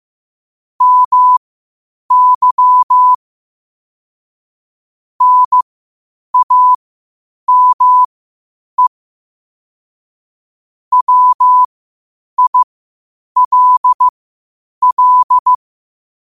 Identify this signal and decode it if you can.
Morse code, heard as a keyed tone, about 15 wpm, spaced as for 8 wpm, 1000 Hz. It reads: MY NAME WILL
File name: qotdfiles / 8wpm